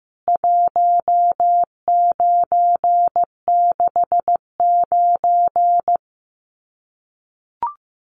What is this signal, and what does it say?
Morse code, heard as a keyed tone, about 15 wpm, 700 Hz. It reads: 1969
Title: CWT Exchange 15wpm